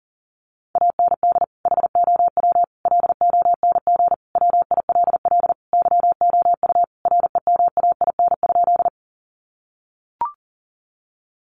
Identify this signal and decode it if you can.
Morse code, heard as a keyed tone, about 40 wpm, 700 Hz. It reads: AND HOW LONG WILL YOU REMAIN?
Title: Sentences from Top 1200 Words 40wpm